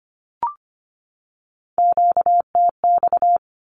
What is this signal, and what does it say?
Morse code, heard as a keyed tone, about 25 wpm, 700 Hz. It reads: QTX
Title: Q-Codes 25wpm